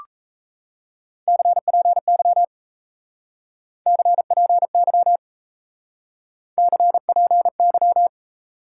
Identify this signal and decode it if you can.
Morse code, heard as a keyed tone, about 37 wpm, 700 Hz. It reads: CPY CPY CPY